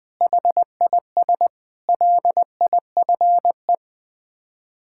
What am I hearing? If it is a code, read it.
Morse code, heard as a keyed tone, about 20 wpm, 700 Hz. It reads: HIS LIFE
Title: Sets of 6 Words - 20wpm